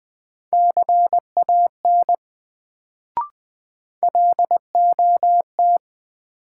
Morse code, heard as a keyed tone, about 20 wpm, 700 Hz. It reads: CAN LOT
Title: Sets of 2 Words - New Word Drill - 20wpm